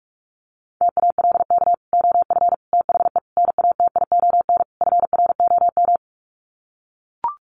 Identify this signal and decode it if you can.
Morse code, heard as a keyed tone, about 44 wpm, 700 Hz. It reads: TALK OF THE NATION FROM
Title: Sets of 5 Words 45wpm